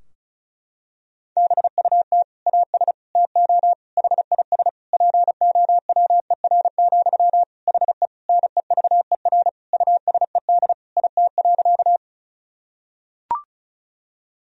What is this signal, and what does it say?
Morse code, heard as a keyed tone, about 36 wpm, 700 Hz. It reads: BUT AS TO HIS POWER, HE NEVER USED IT.
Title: Sentences from Top 1000 Words - Review Part 2 - 35wpm